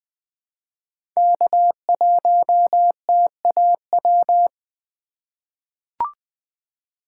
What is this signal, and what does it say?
Morse code, heard as a keyed tone, about 20 wpm, 700 Hz. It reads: K1TAW
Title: USA Call Signs - 5 Characters - 2x Character Spacing - 20wpm